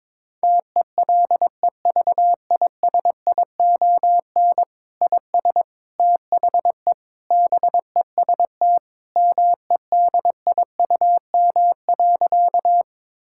Morse code, heard as a keyed tone, about 22 wpm, 700 Hz. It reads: TELEVISION IS THE BEST MEDIUM.
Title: Sentences from Top 1000 Words - Review Part 6 - 22wpm